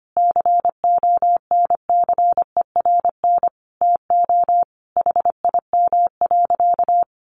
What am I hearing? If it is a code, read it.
Morse code, heard as a keyed tone, about 25 wpm, 700 Hz. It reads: CONCERN TO HIM.
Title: Sentences from Top 1200 Words 25wpm